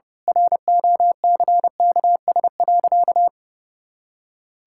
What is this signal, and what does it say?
Morse code, heard as a keyed tone, about 30 wpm, 700 Hz. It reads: ROCKS.
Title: Pangram Challenge v2 - 30wpm